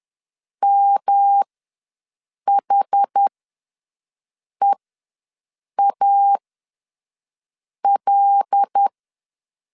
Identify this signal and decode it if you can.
Morse code, heard as a keyed tone, about 10 wpm, spaced as for 5 wpm, 800 Hz. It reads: MHEAL
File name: LEZIONI / 7 GF9Z / 7 G ws 10wpm 3843